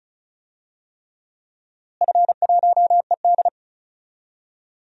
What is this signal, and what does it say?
Morse code, heard as a keyed tone, about 34 wpm, 700 Hz. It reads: F1ED
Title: DXCC-100 Call Signs - 4 Characters - Repeated 3x - 35wpm